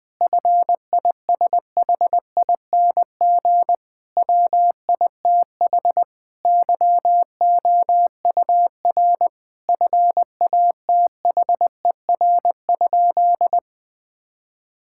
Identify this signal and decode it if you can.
Morse code, heard as a keyed tone, about 20 wpm, 700 Hz. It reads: FISHING WITH YOUR FATHER?
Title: Sentences from Top 1000 Words - Review Part 5 - 20wpm